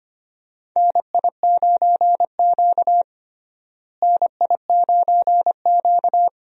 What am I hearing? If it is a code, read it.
Morse code, heard as a keyed tone, about 25 wpm, 700 Hz. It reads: NI9Q NI9Q
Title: USA Call Signs - 4 Characters - Repeated 2x - 25wpm